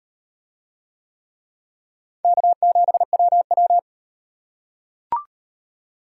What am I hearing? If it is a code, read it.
Morse code, heard as a keyed tone, about 34 wpm, 700 Hz. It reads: K7WW